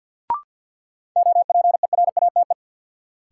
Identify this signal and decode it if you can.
Morse code, heard as a keyed tone, about 50 wpm, 700 Hz. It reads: OPERATE
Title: Sentences from Top 1500 Words - New Word Drill Alt - 50wpm